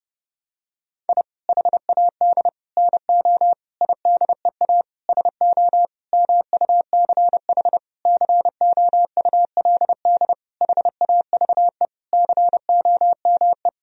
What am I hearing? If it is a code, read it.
Morse code, heard as a keyed tone, about 30 wpm, 700 Hz. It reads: I HAD NO IDEA SO MUCH COULD HAVE COME
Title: Sentences from Top 1000 Words - Review Part 4 - 30wpm